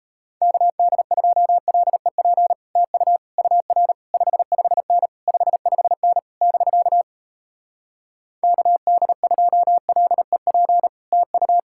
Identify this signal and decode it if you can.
Morse code, heard as a keyed tone, about 37 wpm, 700 Hz. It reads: KD2LEP TU UR 55N 55N <BK> KD2LEP TU